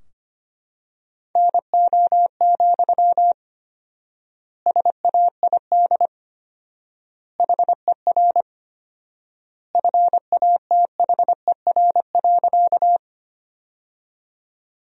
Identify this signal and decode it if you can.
Morse code, heard as a keyed tone, about 25 wpm, 700 Hz. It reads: NO, SAID HER FATHER.
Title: 4x Word-Spacing - Sentences from Top 300 Words - 25wpm